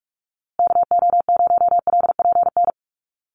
Morse code, heard as a keyed tone, about 44 wpm, 700 Hz. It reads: KO0LPN
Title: USA Call Signs Repeated 3x 45wpm